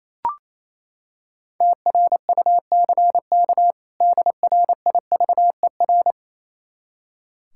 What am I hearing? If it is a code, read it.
Morse code, heard as a keyed tone, about 28 wpm, 700 Hz. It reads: TRUCK DRIVER
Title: Practice QSO - Exchange 3 - More Info - 28wpm